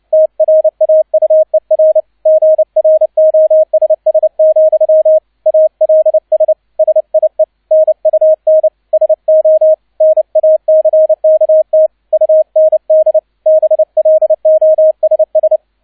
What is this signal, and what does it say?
Morse code, heard as a keyed tone, about 29 wpm, 630 Hz. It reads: TRAUER GROSS, ALS SIE NUN SO NACKT UND BLOSS
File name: 002 Zweiter Streich 30wpm